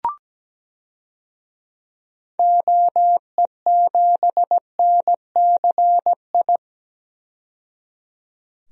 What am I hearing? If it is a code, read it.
Morse code, heard as a keyed tone, about 17 wpm, 700 Hz. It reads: OE7NCI
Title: DXCC-100 Call Signs - 6 Characters - 17wpm